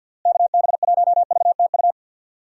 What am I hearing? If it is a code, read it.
Morse code, heard as a keyed tone, about 50 wpm, 700 Hz. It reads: KB1VTU
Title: USA Call Signs - 6 Characters - Repeated 3x - 50wpm